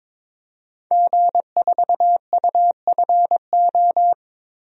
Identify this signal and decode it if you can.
Morse code, heard as a keyed tone, about 22 wpm, 700 Hz. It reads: G4UFO
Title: DXCC-100 Call Signs - 5 Characters - Repeated 2x - 22wpm